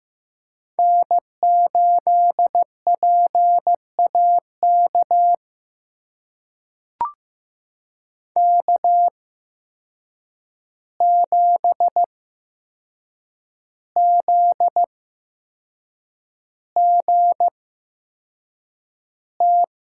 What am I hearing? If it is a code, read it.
Morse code, heard as a keyed tone, about 15 wpm, spaced as for 6 wpm, 700 Hz. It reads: N8PAK K7ZGT
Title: USA Call Signs - 5 Characters - 8x Character Spacing - 15wpm